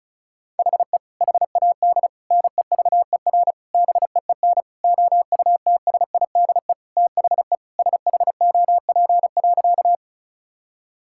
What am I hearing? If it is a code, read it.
Morse code, heard as a keyed tone, about 36 wpm, 700 Hz. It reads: HE HAD NEVER BEEN OUTSIDE THE SHOP.
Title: Sentences from Top 1600 Words 35wpm